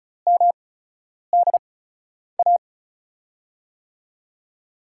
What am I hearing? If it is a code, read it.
Morse code, heard as a keyed tone, about 34 wpm, spaced as for 10 wpm, 700 Hz. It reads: MDA
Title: USA Call Signs - 6 Characters - 8x Character Spacing - 35wpm